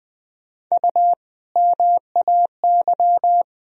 Morse code, heard as a keyed tone, about 21 wpm, 700 Hz. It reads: U MAY
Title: Sentences from Top 100 Words and Abbreviations 20wpm